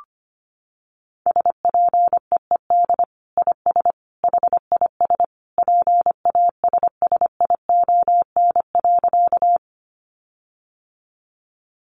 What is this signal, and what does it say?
Morse code, heard as a keyed tone, about 25 wpm, 700 Hz. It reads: SPEED IS HIS PASSION.
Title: Sentences from Top 1300 Words 25wpm